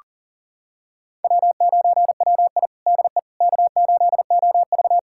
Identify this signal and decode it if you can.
Morse code, heard as a keyed tone, about 40 wpm, 700 Hz. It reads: W9WI DE K8OV
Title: USA Callsign DE Callsign - 4 Characters - 40wpm